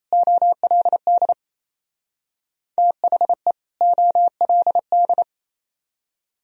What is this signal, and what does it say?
Morse code, heard as a keyed tone, about 35 wpm, 700 Hz. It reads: OLD THE OLD